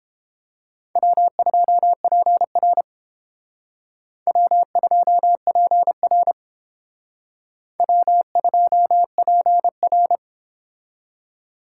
Morse code, heard as a keyed tone, about 31 wpm, 700 Hz. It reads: W2PR W2PR W2PR